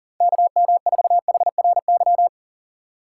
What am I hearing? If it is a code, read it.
Morse code, heard as a keyed tone, about 38 wpm, 700 Hz. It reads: KM4HRY
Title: USA Call Signs - 6 Characters - 4x Character Spacing - 40wpm